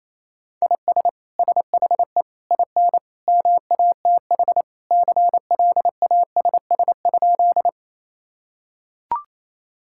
Morse code, heard as a keyed tone, about 28 wpm, 700 Hz. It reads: IS SHE IN MATH CLASS?
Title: Sentences from Top 1000 Words - Review Part 6 - 28wpm